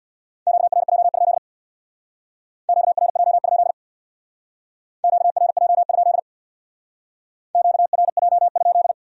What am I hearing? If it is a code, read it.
Morse code, heard as a keyed tone, about 61 wpm, 700 Hz. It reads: QRJ? QRJ? QRJ? QRJ?
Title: Speed-Racing Q-Codes - 50wpm